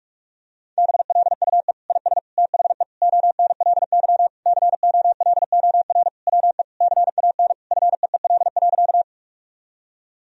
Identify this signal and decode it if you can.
Morse code, heard as a keyed tone, about 46 wpm, 700 Hz. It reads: BLUE IS THE ONLY COLOR WE CAN FEEL.
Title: Sentences from Top 1000 Words - Review Part 3 - 45wpm